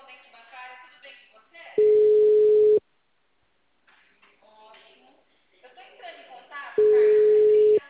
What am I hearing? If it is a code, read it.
Ringback tone (Europe, 425 Hz), with speech.